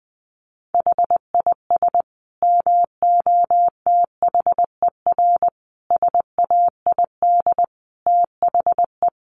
Morse code, heard as a keyed tone, about 20 wpm, 700 Hz. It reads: HIS MOTHER SAID THE
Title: Sentences from Top 800 Words 20wpm